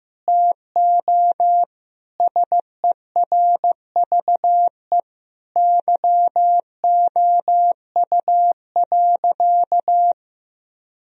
Morse code, heard as a keyed tone, about 15 wpm, 700 Hz. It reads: TO SERVE YOU.
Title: Sentences from Top 1000 Words - Review Part 2 - 15wpm